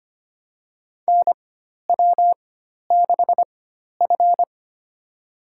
Morse code, heard as a keyed tone, about 25 wpm, spaced as for 12 wpm, 700 Hz. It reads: NW6F
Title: USA Call Signs - 4x Character Spacing - 25wpm